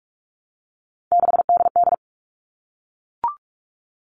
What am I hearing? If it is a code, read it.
Morse code, heard as a keyed tone, about 44 wpm, 700 Hz. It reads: 6DD
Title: All Cut Numbers - 3-Digit 45wpm